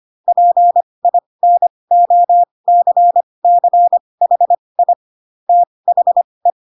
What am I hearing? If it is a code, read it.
Morse code, heard as a keyed tone, about 25 wpm, spaced as for 20 wpm, 700 Hz. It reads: PINOCCHI THE